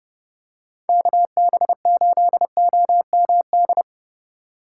Morse code, heard as a keyed tone, about 28 wpm, 700 Hz. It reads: KB8OMD